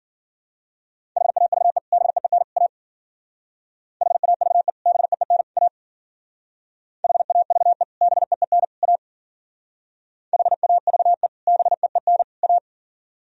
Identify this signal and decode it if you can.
Morse code, heard as a keyed tone, about 50 wpm, 700 Hz. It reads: HAVE BEEN A HAVE BEEN A HAVE BEEN A HAVE BEEN A